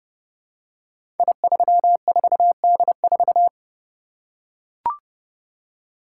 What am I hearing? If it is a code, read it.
Morse code, heard as a keyed tone, about 29 wpm, 700 Hz. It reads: I34D4
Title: Code Groups - 5 Letter:Numbers - 30wpm